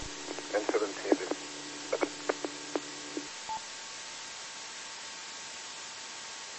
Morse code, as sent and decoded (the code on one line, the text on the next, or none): none